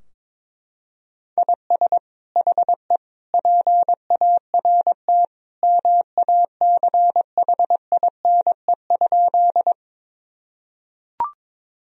Morse code, heard as a keyed tone, about 22 wpm, 700 Hz. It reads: IS HE PART MACHINE?
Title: Sentences from Top 1000 Words - Review Part 2 - 22wpm